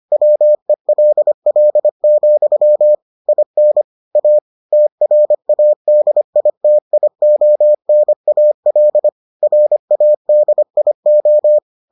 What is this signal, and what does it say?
Morse code, heard as a keyed tone, about 25 wpm, 600 Hz. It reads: WELL, IN A TRADITIONAL RADIO